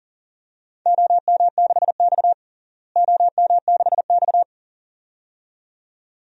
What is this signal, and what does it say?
Morse code, heard as a keyed tone, about 40 wpm, 700 Hz. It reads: OM6X OM6X
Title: DXCC-100 Call Signs - 4 Characters - Repeated 2x - 40wpm